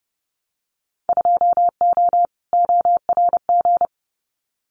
Morse code, heard as a keyed tone, about 30 wpm, 700 Hz. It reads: 2O ORG
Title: Winter Field Day Exchanges 30wpm